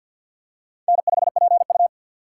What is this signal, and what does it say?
Morse code, heard as a keyed tone, about 53 wpm, 700 Hz. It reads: N5PU
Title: K1USN SST QSO Practice - 50wpm